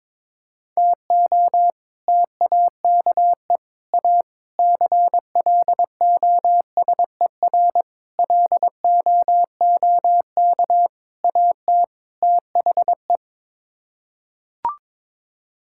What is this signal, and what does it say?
Morse code, heard as a keyed tone, about 22 wpm, 700 Hz. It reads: TO TAKE A CLOSER LOOK AT THE
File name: practice / 22wpm / Sets of 7 Words - 22wpm